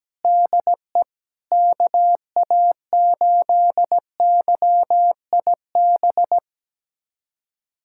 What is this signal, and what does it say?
Morse code, heard as a keyed tone, about 17 wpm, 700 Hz. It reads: DE KA8YIB
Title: USA Callsign DE Callsign 17wpm